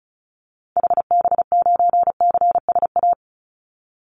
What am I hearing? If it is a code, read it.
Morse code, heard as a keyed tone, about 34 wpm, 700 Hz. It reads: HB9CSA
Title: Call Signs - 35wpm